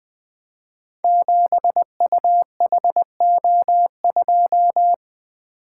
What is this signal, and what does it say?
Morse code, heard as a keyed tone, about 20 wpm, 700 Hz. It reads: 7UHO2
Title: Code Groups - 5 Letter:Numbers - 20wpm